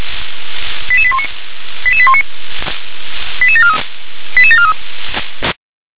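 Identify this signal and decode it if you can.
Five-tone selective call (ZVEI), heard as quick runs of five tones, tones of about 70 ms. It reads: 88819 88743 88743
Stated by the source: ZVEI five-tone call received by SDR.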